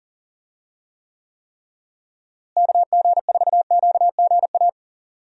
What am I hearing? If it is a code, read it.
Morse code, heard as a keyed tone, about 40 wpm, 700 Hz. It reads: KG4QGA
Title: USA Call Signs - 6 Characters - Repeated 3x - 40wpm